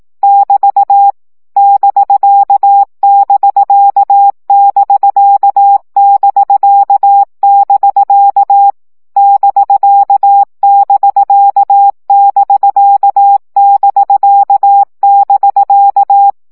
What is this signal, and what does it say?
Morse code, heard as a keyed tone, about 18 wpm, 800 Hz. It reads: <BT> <BK><BK><BK><BK><BK> <BK><BK><BK><BK><BK>
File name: CW gyakorlas bk 15 baud